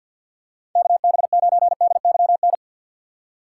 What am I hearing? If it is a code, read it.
Morse code, heard as a keyed tone, about 52 wpm, 700 Hz. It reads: KB9DYN
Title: USA Call Signs - 6 Characters - 4x Character Spacing - 50wpm